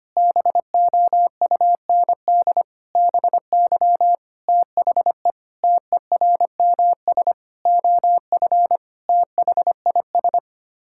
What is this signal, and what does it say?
Morse code, heard as a keyed tone, about 25 wpm, 700 Hz. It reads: BOUND BY THE TERMS OF THIS
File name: Sets of 9 Words - 25wpm